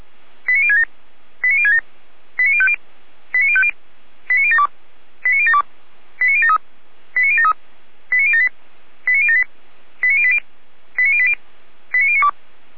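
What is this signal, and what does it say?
Five-tone selective call (ZVEI), heard as quick runs of five tones, tones of about 80 ms. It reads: 89067 89067 89050 89050 89072 89072 89073 89073 89078 89078 89080 89080 89092
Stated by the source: ZVEI five-tone selective call received by SDR, from the Feuerwehr Orlamünde fire brigade audio archive.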